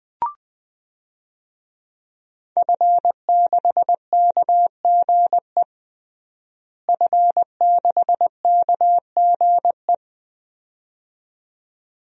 Morse code, heard as a keyed tone, about 20 wpm, 700 Hz. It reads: F6KGE F6KGE
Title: DXCC-100 Call Signs - 5 Characters - Repeated 2x - 20wpm